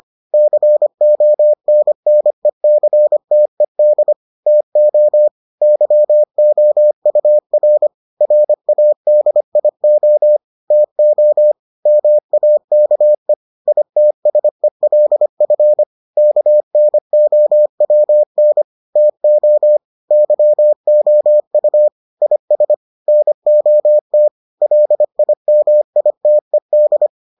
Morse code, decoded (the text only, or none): CONNECTED TO YOUR RADIO TO MAKE ITSELF KNOWN TO YOU IS NOT LIMITED